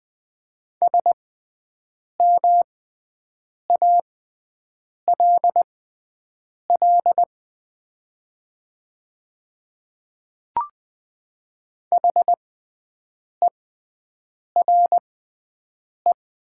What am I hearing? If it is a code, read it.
Morse code, heard as a keyed tone, about 20 wpm, spaced as for 7 wpm, 690 Hz. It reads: SMALL HERE
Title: 6x Character Spacing - Top 200 Words - 20wpm